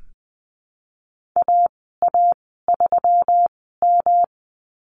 Morse code, heard as a keyed tone, about 20 wpm, spaced as for 15 wpm, 700 Hz. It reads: AA3M
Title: 161 Lesson 26g - Letter g - Call Signs 20wpm